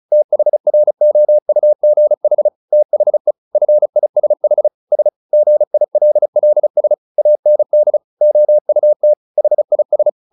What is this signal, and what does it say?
Morse code, heard as a keyed tone, about 34 wpm, 600 Hz. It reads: THROUGH THE FISH S GILLS AND OUT HIS
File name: the old man and the sea 2 35wpm